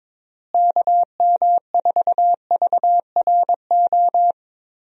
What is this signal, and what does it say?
Morse code, heard as a keyed tone, about 22 wpm, 700 Hz. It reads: KM4VRO
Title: USA Call Signs - 6 Characters - Repeated 3x - 22wpm